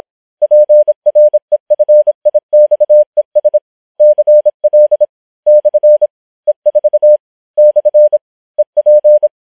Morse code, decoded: PREFIXES CL / E4 / EP